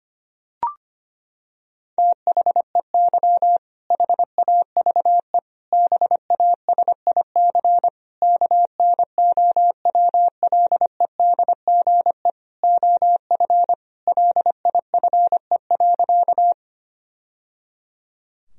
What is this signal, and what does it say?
Morse code, heard as a keyed tone, about 25 wpm, 700 Hz. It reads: THEY HAVE BASIC KNOWLEDGE OF LIFE.
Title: Sentences from Top 1000 Words - Review Part 3 - 25wpm